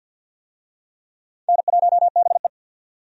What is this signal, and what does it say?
Morse code, heard as a keyed tone, about 50 wpm, 700 Hz. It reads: N1BE
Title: USA Call Signs - 4 Characters - 50wpm